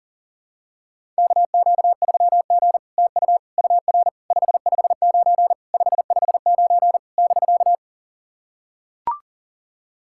Morse code, heard as a keyed tone, about 41 wpm, 700 Hz. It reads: KQ3G TU UR 559 559 <BK>
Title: SOTA QSO Practice 40wpm